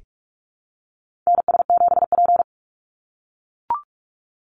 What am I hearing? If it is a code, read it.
Morse code, heard as a keyed tone, about 44 wpm, 700 Hz. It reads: NS7P
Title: NA SOTA Activator Callsigns 45wpm